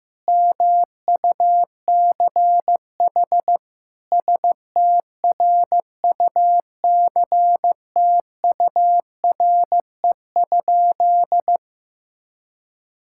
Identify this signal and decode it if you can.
Morse code, heard as a keyed tone, about 14 wpm, 700 Hz. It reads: MUCH STRUCTURE?
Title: Sentences from Top 1000 Words - Review Part 6 - 15wpm